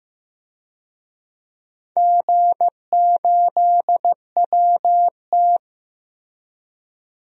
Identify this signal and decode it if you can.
Morse code, heard as a keyed tone, about 15 wpm, 700 Hz. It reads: G8WT
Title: DXCC-100 Call Signs - 4 Characters - 15wpm